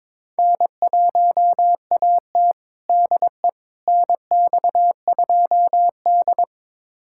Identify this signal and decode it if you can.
Morse code, heard as a keyed tone, about 21 wpm, 700 Hz. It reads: N1AT DE NX2D
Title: USA Callsign DE Callsign - 4 Characters - 22wpm